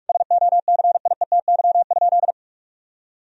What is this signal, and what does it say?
Morse code, heard as a keyed tone, about 46 wpm, 700 Hz. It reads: SOCIETY?
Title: Sentences from Top 1000 Words - Review Part 4 - 45wpm